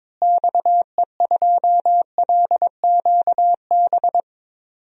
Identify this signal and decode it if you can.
Morse code, heard as a keyed tone, about 22 wpm, 700 Hz. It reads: XE2LQB
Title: DXCC-100 Call Signs - 6 Characters - Repeated 3x - 22wpm